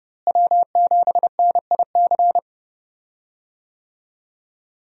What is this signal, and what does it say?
Morse code, heard as a keyed tone, about 30 wpm, 700 Hz. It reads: W7NIC
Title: USA Call Signs - 5 Characters - Repeated 3x - 30wpm